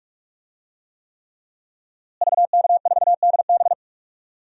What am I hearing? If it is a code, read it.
Morse code, heard as a keyed tone, about 46 wpm, 700 Hz. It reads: VK4DB
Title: DXCC-100 Call Signs - 5 Characters - Repeated 3x - 45wpm